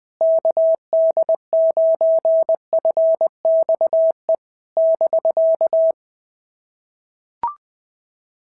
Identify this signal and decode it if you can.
Morse code, heard as a keyed tone, about 20 wpm, 660 Hz. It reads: KD9FXE <BK>
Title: SOTA QSO Practice 20wpm